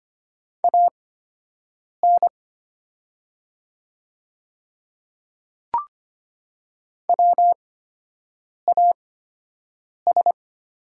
Morse code, heard as a keyed tone, about 25 wpm, spaced as for 7 wpm, 700 Hz. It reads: AN WAS
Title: 8x Character Spacing - Top 100 Words - 25wpm